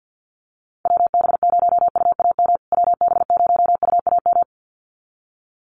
Morse code, heard as a keyed tone, about 50 wpm, 700 Hz. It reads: WB0UAM WB0UAM
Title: USA Call Signs Repeated 2x 50wpm